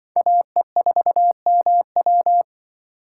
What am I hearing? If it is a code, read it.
Morse code, heard as a keyed tone, about 24 wpm, 700 Hz. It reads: AE4MW
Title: Speed Racing - USA Call Signs 20wpm